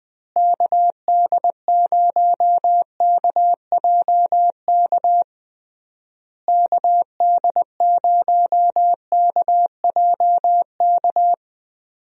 Morse code, heard as a keyed tone, about 21 wpm, 700 Hz. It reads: KD0KJK KD0KJK
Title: USA Call Signs - 6 Characters - Repeated 3x - 20wpm